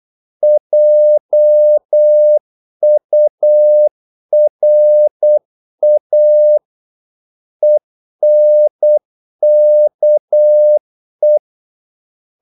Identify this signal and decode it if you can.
Morse code, heard as a keyed tone, about 8 wpm, 600 Hz. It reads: JURA ENKE
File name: Korte ord DK 8wpm